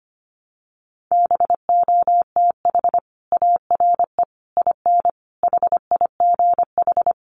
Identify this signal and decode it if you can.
Morse code, heard as a keyed tone, about 25 wpm, 700 Hz. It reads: BOTH ARE IN HIGH
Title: Sentences from Top 1200 Words 25wpm